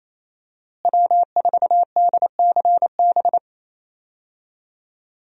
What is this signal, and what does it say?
Morse code, heard as a keyed tone, about 27 wpm, 700 Hz. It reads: W4DCB
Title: USA Call Signs - 5 Characters - Repeated 3x - 28wpm